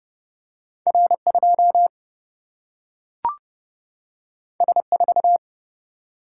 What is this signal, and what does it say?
Morse code, heard as a keyed tone, about 30 wpm, 700 Hz. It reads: R 2 S 4
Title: Single-Letter Single-Number 30wpm